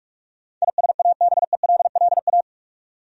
Morse code, heard as a keyed tone, about 46 wpm, 700 Hz. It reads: ISABELLA
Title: USA Common Names - 45wpm